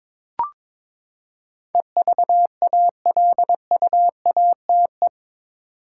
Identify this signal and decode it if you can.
Morse code, heard as a keyed tone, about 22 wpm, 700 Hz. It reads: EVALUATE
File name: practice / 22wpm / Eight Letter Words - 22wpm